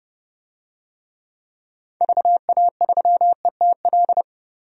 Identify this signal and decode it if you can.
Morse code, heard as a keyed tone, about 30 wpm, 700 Hz. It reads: VA3ETL
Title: DXCC-100 Call Signs - 6 Characters - 30wpm